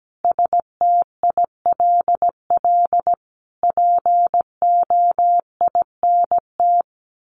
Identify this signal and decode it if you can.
Morse code, heard as a keyed tone, about 17 wpm, 700 Hz. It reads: STILL POINT
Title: Sentences from Top 1500 Words 17wpm